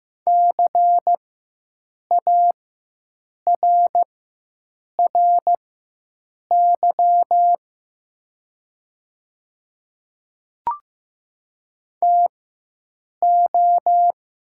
Morse code, heard as a keyed tone, about 15 wpm, spaced as for 7 wpm, 700 Hz. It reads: CARRY TO
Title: 4x Character Spacing - Top 300 Words - 15wpm